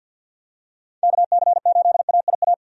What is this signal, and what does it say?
Morse code, heard as a keyed tone, about 51 wpm, 700 Hz. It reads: KC8AIA
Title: USA Call Signs - 6 Characters - 50wpm